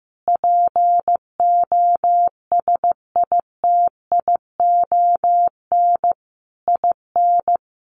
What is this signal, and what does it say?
Morse code, heard as a keyed tone, about 15 wpm, 700 Hz. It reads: POSITION IN
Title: Sentences from Top 1200 Words 15wpm